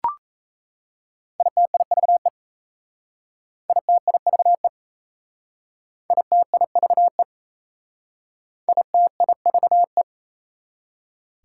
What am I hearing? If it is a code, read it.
Morse code, heard as a keyed tone, about 33 wpm, 690 Hz. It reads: ITIVE ITIVE ITIVE ITIVE
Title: Speed-Racing Prefixes and Suffixes 28wpm